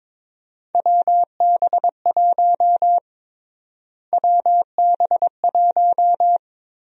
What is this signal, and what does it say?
Morse code, heard as a keyed tone, about 21 wpm, 700 Hz. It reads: WB1 WB1
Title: USA Call Sign Prefixes - Repeated 2x - 22wpm